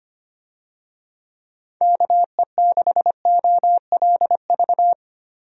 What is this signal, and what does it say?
Morse code, heard as a keyed tone, about 25 wpm, 700 Hz. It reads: KE6OLV
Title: USA Call Signs - 6 Characters - Repeated 3x - 25wpm